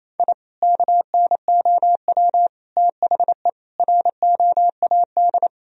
Morse code, heard as a keyed tone, about 28 wpm, 700 Hz. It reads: I KNOW THE ROAD
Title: Sentences from Top 1000 Words - Review Part 2 - 28wpm